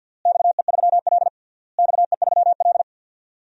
Morse code, heard as a keyed tone, about 51 wpm, 700 Hz. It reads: XE3L XE3L
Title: DXCC-100 Call Signs - 4 Characters - Repeated 3x - 50wpm